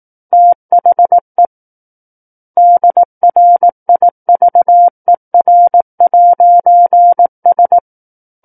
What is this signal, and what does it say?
Morse code, heard as a keyed tone, about 18 wpm, 700 Hz. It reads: THE DRIVER'S
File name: 18 WPM